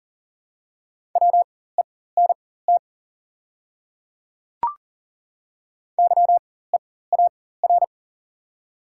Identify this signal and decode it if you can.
Morse code, heard as a keyed tone, about 40 wpm, spaced as for 19 wpm, 700 Hz. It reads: WENT YEAR
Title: 4x Character Spacing - Top 200 Words - 40wpm